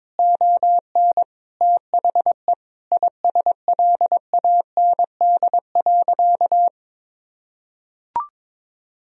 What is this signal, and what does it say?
Morse code, heard as a keyed tone, about 21 wpm, 700 Hz. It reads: ON THE ISLAND.
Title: Sentences from Top 1000 Words - Review Part 3 - 22wpm